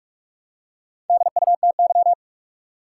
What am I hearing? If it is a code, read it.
Morse code, heard as a keyed tone, about 46 wpm, 700 Hz. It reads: DUTY
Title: Four Letter Words - 45wpm